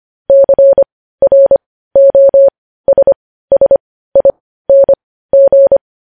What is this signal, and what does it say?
Morse code, heard as a keyed tone, about 25 wpm, spaced as for 15 wpm, 550 Hz. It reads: CROSSING
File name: podcast / News headlines-1x-25x15